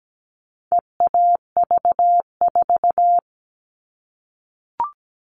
Morse code, heard as a keyed tone, about 17 wpm, 700 Hz. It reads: EAV4
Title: All Cut Numbers - 4-Digit 17wpm